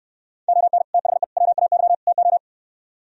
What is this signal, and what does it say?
Morse code, heard as a keyed tone, about 68 wpm, 700 Hz. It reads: ON THE WAY TO